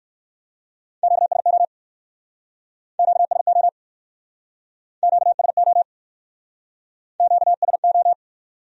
Morse code, heard as a keyed tone, about 61 wpm, 700 Hz. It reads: QSO QSO QSO QSO